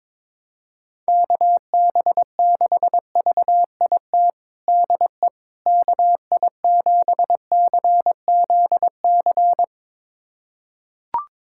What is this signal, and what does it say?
Morse code, heard as a keyed tone, about 22 wpm, 700 Hz. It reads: KB6VIT DE KI7CZC
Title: USA Callsign DE Callsign - 6 Characters - 22wpm